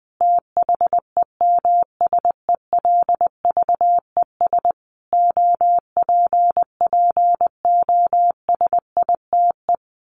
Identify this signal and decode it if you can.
Morse code, heard as a keyed tone, about 20 wpm, 700 Hz. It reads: THEMSELVES OPPOSITE
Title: A Sound Of Thunder 20wpm